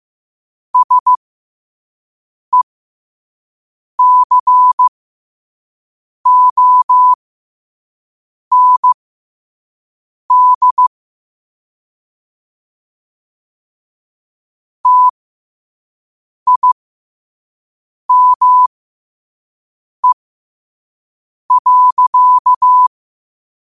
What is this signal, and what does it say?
Morse code, heard as a keyed tone, about 15 wpm, spaced as for 5 wpm, 1000 Hz. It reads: SECOND TIME.
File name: qotdfiles / 5wpm